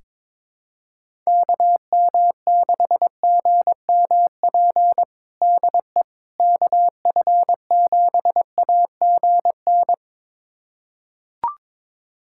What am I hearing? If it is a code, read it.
Morse code, heard as a keyed tone, about 21 wpm, 700 Hz. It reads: KM6GMP DE KF7AGN
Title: USA Callsign DE Callsign - 6 Characters - 22wpm